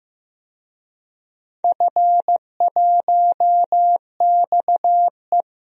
Morse code, heard as a keyed tone, about 15 wpm, 700 Hz. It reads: F1XE
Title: DXCC-100 Call Signs - 4 Characters - 15wpm